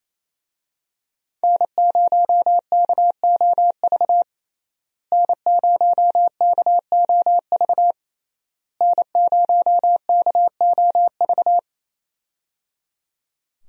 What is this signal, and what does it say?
Morse code, heard as a keyed tone, about 28 wpm, 700 Hz. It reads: N0KOV N0KOV N0KOV
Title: USA Call Signs - 5 Characters - Repeated 3x - 28wpm